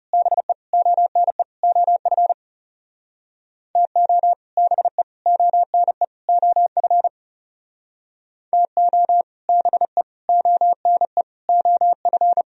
Morse code, heard as a keyed tone, about 34 wpm, 700 Hz. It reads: BE ONE OF TO BE ONE OF TO BE ONE OF
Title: Speed-Racing - Sets of 4 Words 30wpm